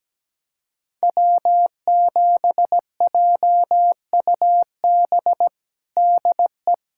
Morse code, heard as a keyed tone, about 17 wpm, 700 Hz. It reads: W7JUB DE
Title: USA Callsign DE Callsign 17wpm